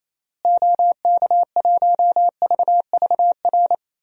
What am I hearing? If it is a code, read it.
Morse code, heard as a keyed tone, about 28 wpm, 700 Hz. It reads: OK1VVR
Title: DXCC-100 Call Signs - 6 Characters - Repeated 3x - 28wpm